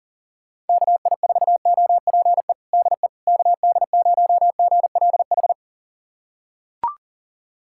Morse code, heard as a keyed tone, about 40 wpm, 700 Hz. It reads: KI4OPE DE KD0GLH
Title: USA Callsign DE Callsign 40wpm